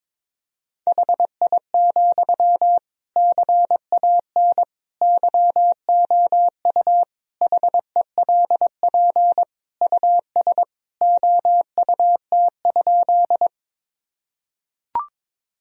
Morse code, heard as a keyed tone, about 22 wpm, 690 Hz. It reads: HI, CAN YOU HELP US OUT?
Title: Sentences from Top 1000 Words - Review Part 3 - 22wpm